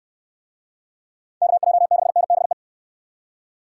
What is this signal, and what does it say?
Morse code, heard as a keyed tone, about 67 wpm, 700 Hz. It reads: KJ6NBE